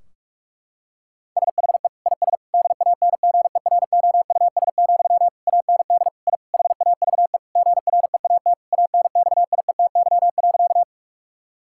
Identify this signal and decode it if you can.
Morse code, heard as a keyed tone, about 44 wpm, 700 Hz. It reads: SHE IS DANGEROUS, AND I HAVE GREAT ANXIETY.
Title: Sentences from Top 1600 Words 45wpm